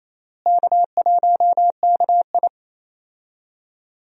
Morse code, heard as a keyed tone, about 28 wpm, 700 Hz. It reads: K1KI
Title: USA Call Signs - 4 Characters - Repeated 2x - 28wpm